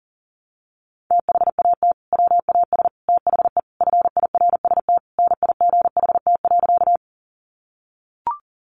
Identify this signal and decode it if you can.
Morse code, heard as a keyed tone, about 40 wpm, 700 Hz. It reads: THAT WAS THE FIRST NIGHT.
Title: Sentences from Top 300 Words 40wpm